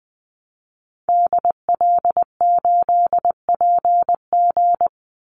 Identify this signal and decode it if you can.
Morse code, heard as a keyed tone, about 20 wpm, 700 Hz. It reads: DL8PG
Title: Call Signs - 20wpm